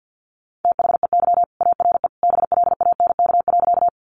Morse code, heard as a keyed tone, about 50 wpm, 700 Hz. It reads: THEY ARE BLANK.
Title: Sentences from Top 1500 Words 50wpm